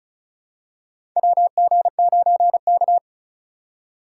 Morse code, heard as a keyed tone, about 36 wpm, 700 Hz. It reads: WG9K
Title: USA Call Signs - 4 Characters - 3x Character Spacing - 35wpm